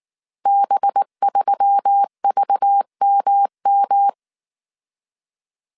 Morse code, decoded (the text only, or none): B3VMM